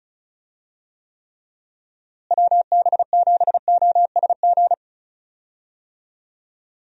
Morse code, heard as a keyed tone, about 34 wpm, 700 Hz. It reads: WB7OSG
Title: USA Call Signs - 6 Characters - Repeated 3x - 35wpm